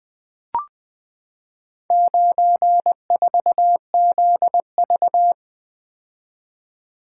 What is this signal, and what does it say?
Morse code, heard as a keyed tone, about 20 wpm, 700 Hz. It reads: 94ZV
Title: Four Letter-Number Combinations - 20wpm